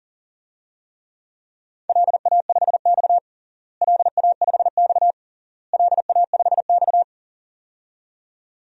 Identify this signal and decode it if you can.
Morse code, heard as a keyed tone, about 40 wpm, 700 Hz. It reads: LA5X LA5X LA5X
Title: DXCC-100 Call Signs - 4 Characters - Repeated 3x - 40wpm